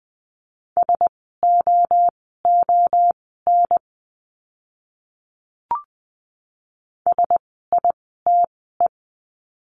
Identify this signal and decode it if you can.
Morse code, heard as a keyed tone, about 20 wpm, spaced as for 15 wpm, 700 Hz. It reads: SOON SITE
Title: 054 Lesson 10f - Letter r - Words 20wpm